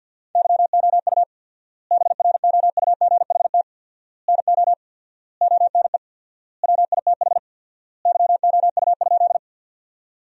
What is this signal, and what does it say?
Morse code, heard as a keyed tone, about 50 wpm, 700 Hz. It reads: YOU BROUGHT NO ONE WITH YOU?
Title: 4x Word-Spacing - Sentences from Top 500 Words - 50wpm